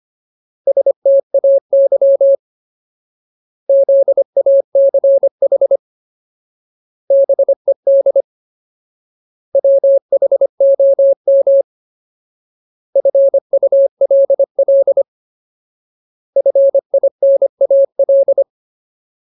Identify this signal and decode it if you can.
Morse code, heard as a keyed tone, about 25 wpm, 550 Hz. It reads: STAY ZACH BED WHOM FULL FINAL
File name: Ultimate Word List / 25WPM Ultimate